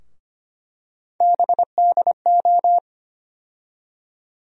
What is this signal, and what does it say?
Morse code, heard as a keyed tone, about 25 wpm, 700 Hz. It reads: BDO
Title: USA Call Sign Postfixes - 3 Letters 25wpm